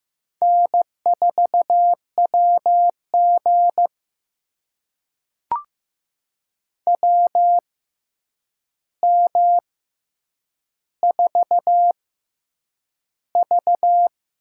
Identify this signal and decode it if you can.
Morse code, heard as a keyed tone, about 15 wpm, spaced as for 8 wpm, 700 Hz. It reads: N4WG WM4V
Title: USA Call Signs - 4 Characters - 6x Character Spacing - 15wpm